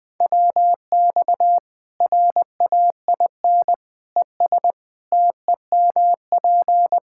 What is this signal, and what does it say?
Morse code, heard as a keyed tone, about 20 wpm, 700 Hz. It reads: WX RAIN ES TEMP
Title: Practice QSO - Exchange 2 - Info - 20wpm